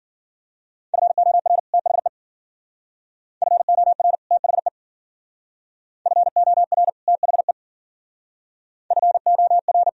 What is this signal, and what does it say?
Morse code, heard as a keyed tone, about 50 wpm, 700 Hz. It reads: FOR THE FOR THE FOR THE FOR